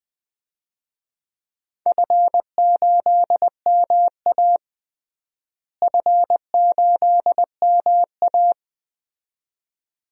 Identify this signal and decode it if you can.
Morse code, heard as a keyed tone, about 20 wpm, 700 Hz. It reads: F8MA F8MA
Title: DXCC-100 Call Signs - 4 Characters - Repeated 2x - 20wpm